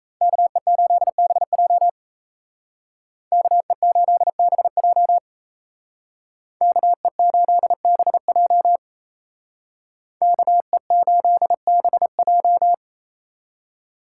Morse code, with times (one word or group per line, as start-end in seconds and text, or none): 0.2-1.9 KE8BJ
3.3-5.2 KE8BJ
6.6-8.8 KE8BJ
10.2-12.7 KE8BJ